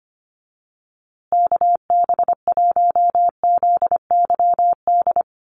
Morse code, heard as a keyed tone, about 25 wpm, 700 Hz. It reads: KB1ZYD
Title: USA Call Signs Repeated 3x 25wpm